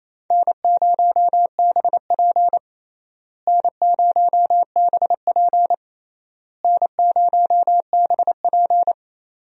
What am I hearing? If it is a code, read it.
Morse code, heard as a keyed tone, about 28 wpm, 700 Hz. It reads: N0BP N0BP N0BP
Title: USA Call Signs - 4 Characters - Repeated 3x - 28wpm